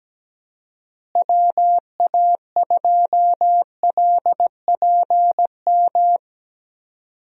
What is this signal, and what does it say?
Morse code, heard as a keyed tone, about 17 wpm, 700 Hz. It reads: WA2LPM
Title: USA Call Signs - 6 Characters - Repeated 3x - 17wpm